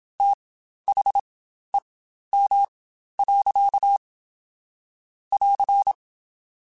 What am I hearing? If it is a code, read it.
Morse code, heard as a keyed tone, about 27 wpm, spaced as for 12 wpm, 800 Hz. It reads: THEM. <AR>
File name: Spaced / Chuckles Spaced-mp3s / SoftWords sp @25